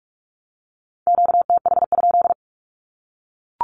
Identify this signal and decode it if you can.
Morse code, heard as a keyed tone, about 44 wpm, 700 Hz. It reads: QTH?
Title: Q-Codes Repeated 3x 45wpm